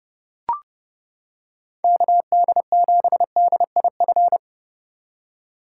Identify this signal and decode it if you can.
Morse code, heard as a keyed tone, about 29 wpm, 700 Hz. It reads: KD7DIF